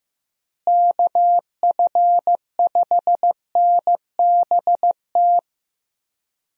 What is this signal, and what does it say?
Morse code, heard as a keyed tone, about 15 wpm, 700 Hz. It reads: KF5NBT
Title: USA Call Signs - 6 Characters - Repeated 3x - 15wpm